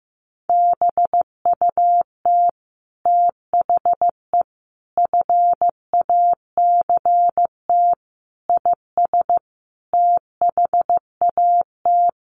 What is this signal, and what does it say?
Morse code, heard as a keyed tone, about 15 wpm, 700 Hz. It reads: BUT THE FACT IS THAT
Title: Sets of 5 Words Encore2 15wpm